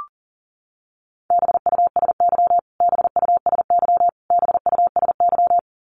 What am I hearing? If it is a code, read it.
Morse code, heard as a keyed tone, about 40 wpm, 700 Hz. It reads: BUSY BUSY BUSY
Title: Top 500 Words Repeated 40wpm